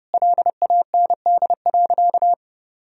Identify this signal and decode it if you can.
Morse code, heard as a keyed tone, about 30 wpm, 700 Hz. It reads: LAND.
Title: Sentences from Top 1000 Words - Review Part 1 - 30wpm